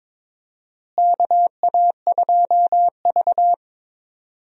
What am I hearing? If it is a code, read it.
Morse code, heard as a keyed tone, about 22 wpm, 700 Hz. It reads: KA2V
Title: USA Call Signs - 4 Characters - Repeated 3x - 22wpm